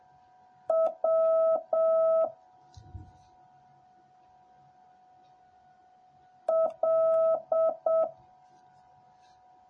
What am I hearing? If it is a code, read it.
Morse code, heard as a keyed tone, about 7 wpm, 650 Hz. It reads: W L